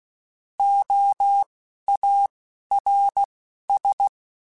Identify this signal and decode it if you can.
Morse code, heard as a keyed tone, about 16 wpm, spaced as for 11 wpm, 800 Hz. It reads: OARS
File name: Alphabetical / Alphabetical mp3s / Os @10